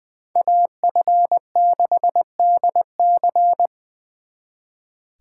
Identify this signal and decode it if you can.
Morse code, heard as a keyed tone, about 23 wpm, 700 Hz. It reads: AF6DC
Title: Speed Racing - USA 2x2 Call Signs 20wpm